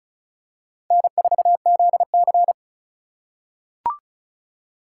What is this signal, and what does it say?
Morse code, heard as a keyed tone, about 34 wpm, 700 Hz. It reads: N4ZC
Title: USA Call Signs - 4 Characters - 4x Character Spacing - 35wpm